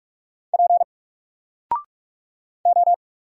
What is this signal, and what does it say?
Morse code, heard as a keyed tone, about 47 wpm, 700 Hz. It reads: P O
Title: Single Character (Letter:Number:Punctuation:Prosign) Alt 45wpm